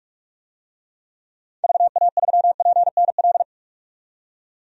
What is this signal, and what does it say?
Morse code, heard as a keyed tone, about 46 wpm, 700 Hz. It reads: VA3PNL
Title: DXCC-100 Call Signs - 6 Characters - 45wpm